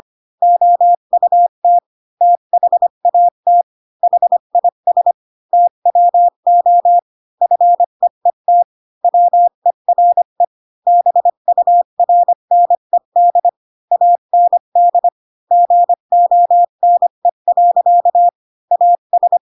Morse code, heard as a keyed tone, about 25 wpm, 700 Hz. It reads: OUT THAT HIS TWO FEET WERE BURNED AND GONE. AS
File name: Pinocchio / Pinocchio ch7 20 25 ik1hge